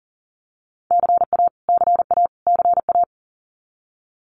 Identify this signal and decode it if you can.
Morse code, heard as a keyed tone, about 40 wpm, 700 Hz. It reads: CA CA CA
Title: US States Repeated 3x 40wpm